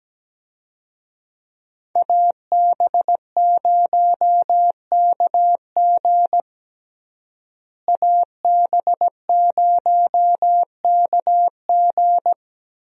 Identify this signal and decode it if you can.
Morse code, heard as a keyed tone, about 17 wpm, 700 Hz. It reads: AB0KG AB0KG
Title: USA Call Signs - 5 Characters - Repeated 3x - 17wpm